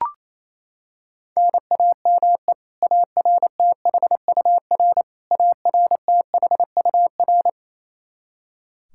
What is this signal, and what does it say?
Morse code, heard as a keyed tone, about 28 wpm, 700 Hz. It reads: NAME ARTHUR ARTHUR
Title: Practice QSO - Exchange 1 - Beginning - 28wpm